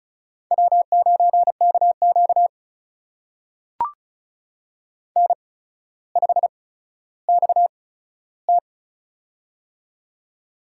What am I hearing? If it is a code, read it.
Morse code, heard as a keyed tone, about 36 wpm, spaced as for 15 wpm, 700 Hz. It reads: W9KQ N5XT
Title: USA Call Signs - 8x Character Spacing - 35wpm